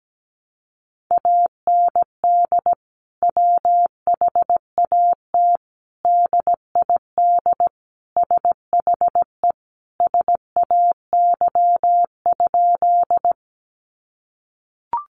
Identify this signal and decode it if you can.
Morse code, heard as a keyed tone, about 17 wpm, 700 Hz. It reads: AND WHAT DID SHE SAY?
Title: Sentences from Top 200 Words 17wpm